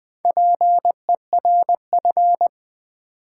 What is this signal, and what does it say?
Morse code, heard as a keyed tone, about 20 wpm, 700 Hz. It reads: PERF
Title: Build A Word - 4 Letters At A Time - 11-Letter Words - 20wpm